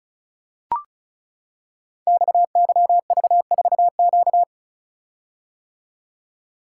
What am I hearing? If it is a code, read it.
Morse code, heard as a keyed tone, about 36 wpm, 700 Hz. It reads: XYV4Q
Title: Code Groups - 5 Letter:Numbers - 35wpm